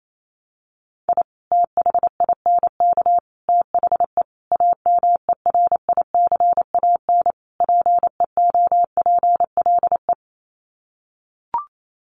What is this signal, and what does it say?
Morse code, heard as a keyed tone, about 28 wpm, 700 Hz. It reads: I THINK THE AMERICAN PEOPLE
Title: Sets of 5 Words Encore 28wpm